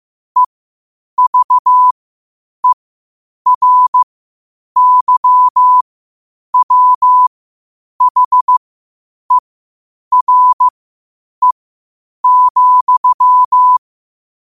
Morse code, heard as a keyed tone, about 15 wpm, spaced as for 8 wpm, 1000 Hz. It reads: EVERYWHERE,
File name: qotdfiles / 8wpm